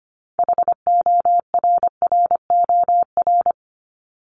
Morse code, heard as a keyed tone, about 25 wpm, 700 Hz. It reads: HORROR
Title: Top 1501-1750 Words 25wpm